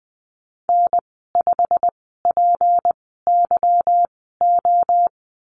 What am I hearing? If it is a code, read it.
Morse code, heard as a keyed tone, about 20 wpm, spaced as for 14 wpm, 700 Hz. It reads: N5PYO
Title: 167 Lesson 27g - Number 7 - Call Signs 20wpm